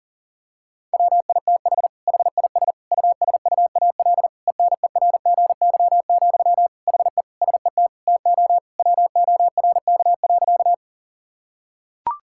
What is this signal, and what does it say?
Morse code, heard as a keyed tone, about 40 wpm, 700 Hz. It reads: WITH HIS USUAL ENERGY, HE SET TO WORK.
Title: Sentences from Top 1000 Words - Review Part 6 - 40wpm